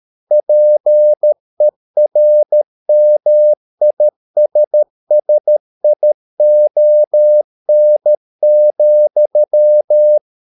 Morse code, heard as a keyed tone, about 13 wpm, 600 Hz. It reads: PERMISSION,